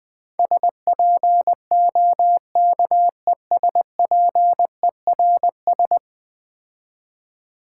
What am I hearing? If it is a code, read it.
Morse code, heard as a keyed tone, about 20 wpm, 700 Hz. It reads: SPOKESPERS
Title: Build A Word - 5 Letters At A Time - 12-Letter Words - 20wpm